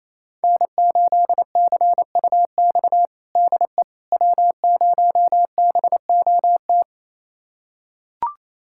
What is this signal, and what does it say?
Morse code, heard as a keyed tone, about 28 wpm, 700 Hz. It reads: N8CUX DE W0BOT
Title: USA Callsign DE Callsign - 5 Characters - 28wpm